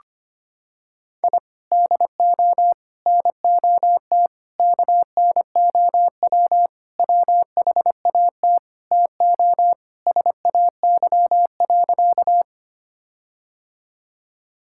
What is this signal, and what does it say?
Morse code, heard as a keyed tone, about 25 wpm, 700 Hz. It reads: I DO NOT KNOW WHAT TO SAY.
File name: practice / 25wpm / Sentences with 4-Word Phrases - 25wpm